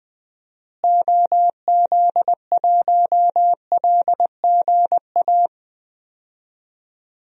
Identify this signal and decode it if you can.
Morse code, heard as a keyed tone, about 20 wpm, 700 Hz. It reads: OZ1LGA
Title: DXCC-100 Call Signs - 6 Characters - Repeated 2x - 20wpm